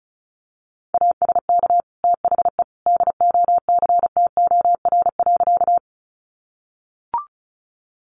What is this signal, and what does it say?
Morse code, heard as a keyed tone, about 34 wpm, 700 Hz. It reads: ASK THE DOCTOR.
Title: Sentences from Top 1500 Words 35wpm